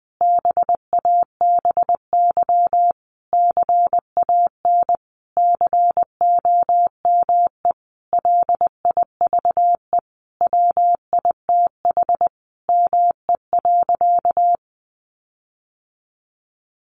Morse code, heard as a keyed tone, about 20 wpm, 700 Hz. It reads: BABY CAN COME LIVE WITH ME.
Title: Sentences from Top 1300 Words 20wpm